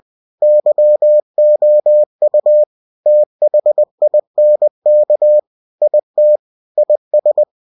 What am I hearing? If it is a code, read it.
Morse code, heard as a keyed tone, about 20 wpm, 600 Hz. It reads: YOU THINK IT IS